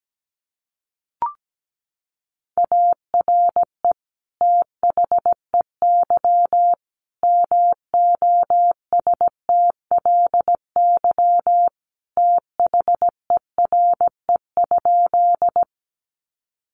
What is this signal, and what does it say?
Morse code, heard as a keyed tone, about 17 wpm, 700 Hz. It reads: ARE THEY MOSTLY THERE?
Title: Sentences from Top 1300 Words 17wpm